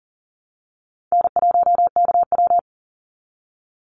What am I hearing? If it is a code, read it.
Morse code, heard as a keyed tone, about 40 wpm, 700 Hz. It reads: N1KW
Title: Easy Call Signs Repeated 3x - 40wpm